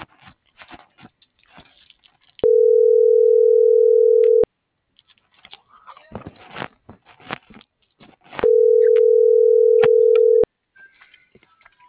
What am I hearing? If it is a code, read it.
Ringback tone (North America), 440 + 480 Hz.
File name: CENTRAL MABE / out edwin.sanchez